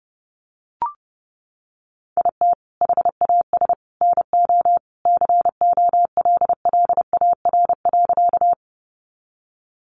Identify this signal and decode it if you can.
Morse code, heard as a keyed tone, about 30 wpm, 700 Hz. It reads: IT HAS NO COLLAR.
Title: Easy Phrases 30wpm